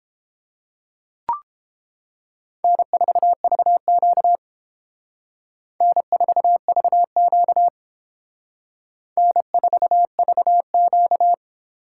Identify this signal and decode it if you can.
Morse code, heard as a keyed tone, about 28 wpm, 700 Hz. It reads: N4VQ N4VQ N4VQ